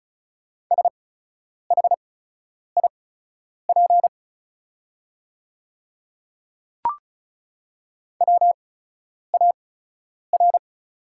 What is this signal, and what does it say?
Morse code, heard as a keyed tone, about 36 wpm, spaced as for 10 wpm, 690 Hz. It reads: SHIP WAR
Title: 8x Character Spacing - Top 500 Words - 35wpm